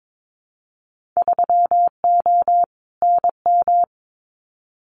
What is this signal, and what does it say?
Morse code, heard as a keyed tone, about 22 wpm, 700 Hz. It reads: 3O NM
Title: Winter Field Day Exchanges Encore 22wpm